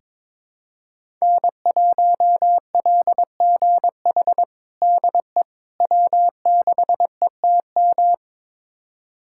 Morse code, heard as a keyed tone, about 22 wpm, 700 Hz. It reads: N1LGH DE W6ETM
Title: USA Callsign DE Callsign - 5 Characters - 22wpm